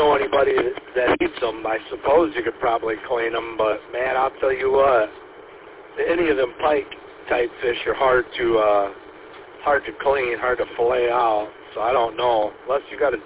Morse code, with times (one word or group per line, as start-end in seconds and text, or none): none